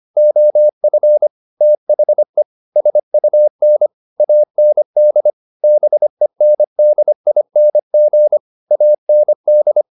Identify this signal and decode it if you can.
Morse code, heard as a keyed tone, about 25 wpm, 600 Hz. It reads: OF THE SUN AND BENDING AND